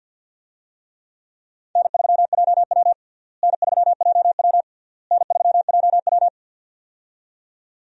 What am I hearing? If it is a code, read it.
Morse code, heard as a keyed tone, about 52 wpm, 700 Hz. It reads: N3JW N3JW N3JW
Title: USA Call Signs - 4 Characters - Repeated 3x - 50wpm